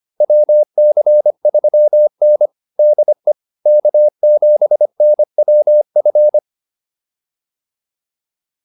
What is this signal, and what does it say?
Morse code, heard as a keyed tone, about 25 wpm, 600 Hz. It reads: WC3N DE K7NWF